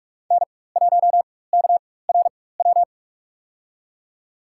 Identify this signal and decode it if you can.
Morse code, heard as a keyed tone, about 41 wpm, spaced as for 20 wpm, 700 Hz. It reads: N1KRW
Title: USA Call Signs - 5 Characters - 4x Character Spacing - 45wpm